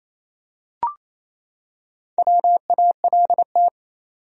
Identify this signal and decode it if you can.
Morse code, heard as a keyed tone, about 27 wpm, 700 Hz. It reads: WALT
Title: USA Common Names - Max 5 Letters - 28wpm